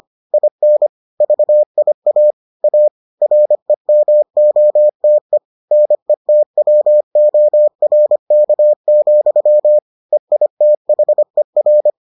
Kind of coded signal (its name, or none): Morse code